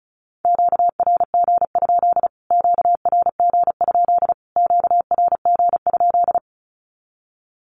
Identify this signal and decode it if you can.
Morse code, heard as a keyed tone, about 34 wpm, 700 Hz. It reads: QRG? QRG? QRG?
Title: Q-Codes Repeated 3x 35wpm